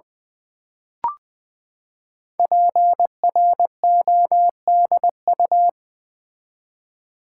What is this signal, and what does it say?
Morse code, heard as a keyed tone, about 20 wpm, 700 Hz. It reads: PRODU
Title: Build A Word - 5 Letters At A Time - 10-Letter Words - 20wpm